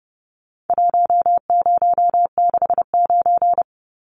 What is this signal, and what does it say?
Morse code, heard as a keyed tone, about 30 wpm, 700 Hz. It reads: 1069
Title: Numbers - Contest Style - 4-Digit 30wpm